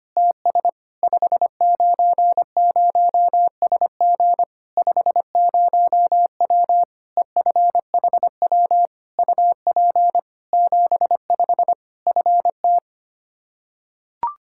Morse code, heard as a keyed tone, about 25 wpm, 700 Hz. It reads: TS 590SG 50W EFHW UP 75 FT
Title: Practice QSO - Exchange 2 - Info - 25wpm